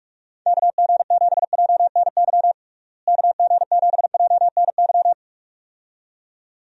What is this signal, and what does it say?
Morse code, heard as a keyed tone, about 46 wpm, 700 Hz. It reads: KG7JNY KG7JNY
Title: USA Call Signs - 6 Characters - Repeated 2x - 45wpm